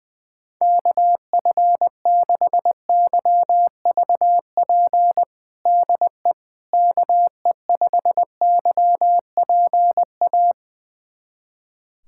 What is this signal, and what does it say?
Morse code, heard as a keyed tone, about 21 wpm, 700 Hz. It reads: KF6YVP DE KE5YPA
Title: USA Callsign DE Callsign 20wpm